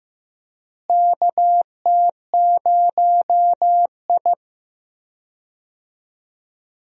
Morse code, heard as a keyed tone, about 15 wpm, 700 Hz. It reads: KT0I
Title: USA Call Signs - 4 Characters - Repeated 2x - 15wpm